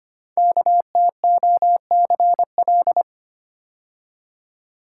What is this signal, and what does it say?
Morse code, heard as a keyed tone, about 25 wpm, 700 Hz. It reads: KTOCL
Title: Code Groups - 5 Letters - 25wpm